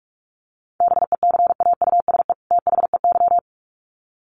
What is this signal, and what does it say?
Morse code, heard as a keyed tone, about 44 wpm, 700 Hz. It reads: BECAUSE THEY
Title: Sets of 2 Words 45wpm